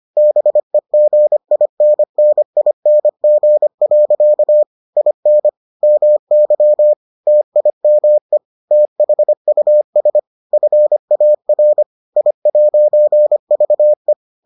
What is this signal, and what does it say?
Morse code, heard as a keyed tone, about 25 wpm, 600 Hz. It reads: BEGINNING. IN MY TIME THUS FAR I'VE